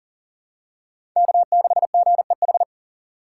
Morse code, heard as a keyed tone, about 40 wpm, 700 Hz. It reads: K6GEH
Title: USA Call Signs - 3x Character Spacing - 40wpm